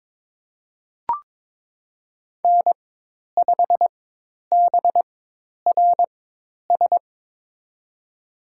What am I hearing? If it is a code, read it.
Morse code, heard as a keyed tone, about 22 wpm, spaced as for 10 wpm, 700 Hz. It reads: N5BRS
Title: USA Call Signs - 5 Characters - 4x Character Spacing - 22wpm